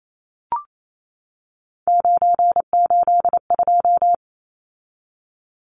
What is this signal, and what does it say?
Morse code, heard as a keyed tone, about 28 wpm, 700 Hz. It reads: 982
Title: Numbers - 3-Digit 28wpm